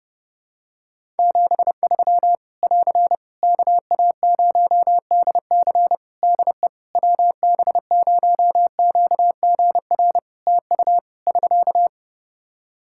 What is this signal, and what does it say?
Morse code, heard as a keyed tone, about 31 wpm, 700 Hz. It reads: 73 <AR> KA0DC DE WB0QGR TU <SK>
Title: Practice QSO - Exchange 4 - Ending - 30wpm